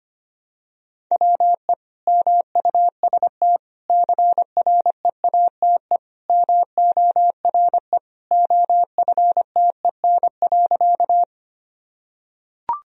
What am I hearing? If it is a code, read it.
Morse code, heard as a keyed tone, about 25 wpm, 700 Hz. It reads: WE MUST CREATE MORE OFTEN.
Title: Sentences from Top 1000 Words - Review Part 3 - 25wpm